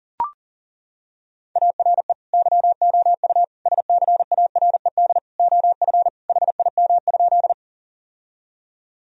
Morse code, heard as a keyed tone, about 40 wpm, 700 Hz. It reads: ARE YOU SCARED OF HIM?
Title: Sentences from Top 1000 Words - Review Part 4 - 40wpm